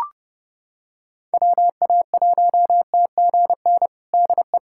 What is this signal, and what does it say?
Morse code, heard as a keyed tone, about 32 wpm, 700 Hz. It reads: WA1TGN DE
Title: USA Callsign DE Callsign - 6 Characters - 30wpm